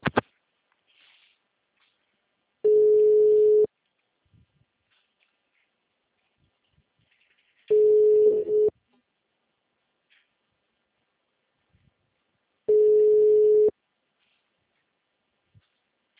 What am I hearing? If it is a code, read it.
Ringback tone (Europe, 425 Hz).
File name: out- -2023- - - .7